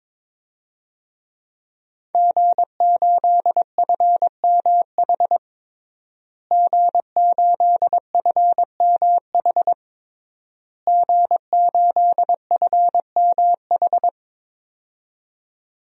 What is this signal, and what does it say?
Morse code, heard as a keyed tone, about 22 wpm, 700 Hz. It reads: G8FMH G8FMH G8FMH
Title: DXCC-100 Call Signs - 5 Characters - Repeated 3x - 22wpm